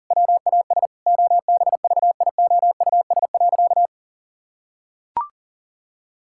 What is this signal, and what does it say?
Morse code, heard as a keyed tone, about 40 wpm, 700 Hz. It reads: WAS OBVIOUS.
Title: Sentences from Top 1600 Words 40wpm